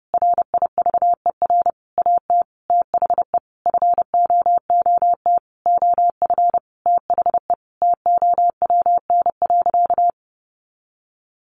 Morse code, heard as a keyed tone, about 30 wpm, 700 Hz. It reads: RIVER AT THE FOOT OF THE TOWN.
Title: Sentences from Top 1400 Words 30wpm